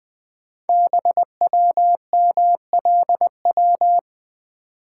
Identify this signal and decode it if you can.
Morse code, heard as a keyed tone, about 20 wpm, 700 Hz. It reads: BWMLW
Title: Code Groups - 5 Letters - 20wpm